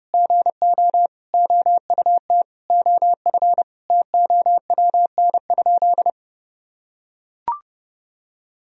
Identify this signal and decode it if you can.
Morse code, heard as a keyed tone, about 30 wpm, 700 Hz. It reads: GO OUT OF TOWN?
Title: Sentences from Top 1000 Words - Review Part 3 - 30wpm